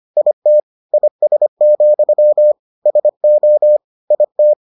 Morse code, heard as a keyed tone, about 25 wpm, 600 Hz. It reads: IT IS, SO IT